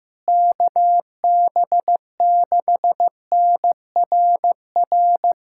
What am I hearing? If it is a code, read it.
Morse code, heard as a keyed tone, about 15 wpm, 700 Hz. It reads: KB6NRR
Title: USA Callsign DE Callsign 15wpm